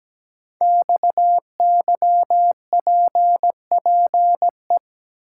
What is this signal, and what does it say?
Morse code, heard as a keyed tone, about 17 wpm, 700 Hz. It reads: XYPPE
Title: Code Groups - 5 Letters - 17wpm